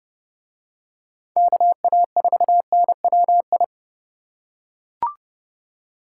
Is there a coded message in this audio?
Yes: Morse code from 1.4 to 3.6 s, heard as a keyed tone, about 28 wpm, 700 Hz. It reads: KA4NWI